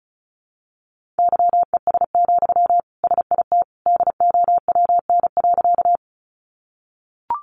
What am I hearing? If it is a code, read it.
Morse code, heard as a keyed tone, about 34 wpm, 700 Hz. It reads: YES, SIT DOWN.
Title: Sentences from Top 500 Words 35wpm